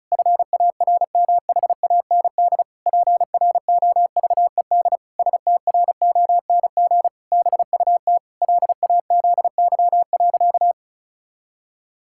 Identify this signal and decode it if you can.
Morse code, heard as a keyed tone, about 36 wpm, 700 Hz. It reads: FARMHAND PROVED STRONG BUT LAZY.
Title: Pangram Challenge v2 - 35wpm